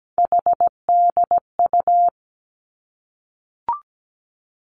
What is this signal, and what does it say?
Morse code, heard as a keyed tone, about 17 wpm, 700 Hz. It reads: HDU
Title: Three Challenging Letter-Number Combinations 17wpm